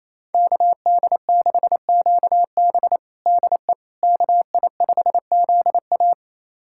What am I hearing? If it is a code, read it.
Morse code, heard as a keyed tone, about 28 wpm, 700 Hz. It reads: KD6QB DE KI5ZA
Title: USA Callsign DE Callsign - 5 Characters - 28wpm